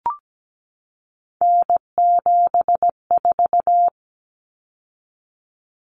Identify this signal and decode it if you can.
Morse code, heard as a keyed tone, about 17 wpm, 700 Hz. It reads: N74
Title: Common Cut Numbers - 3-Digit 17wpm